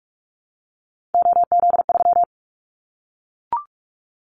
Morse code, heard as a keyed tone, about 44 wpm, 700 Hz. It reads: OZ3
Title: Callsign Prefixes for Top 100 DXCC - 45wpm